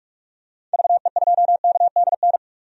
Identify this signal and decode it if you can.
Morse code, heard as a keyed tone, about 41 wpm, 700 Hz. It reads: VE2KDN
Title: DXCC-100 Call Signs - 6 Characters - Repeated 3x - 45wpm